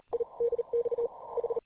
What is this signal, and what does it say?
Morse code, heard as a keyed tone, about 37 wpm, spaced as for 32 wpm, 470 Hz. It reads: IDX H